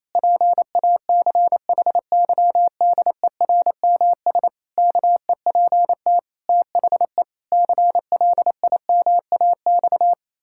Morse code, heard as a keyed tone, about 27 wpm, 700 Hz. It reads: PACHYDERMS KEPT THE CLIMAX
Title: Pangram Challenge v2 - 28wpm